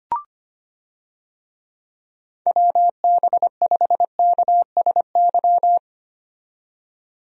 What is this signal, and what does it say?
Morse code, heard as a keyed tone, about 25 wpm, 700 Hz. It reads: WB5KSY
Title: USA Call Signs - 6 Characters - 25wpm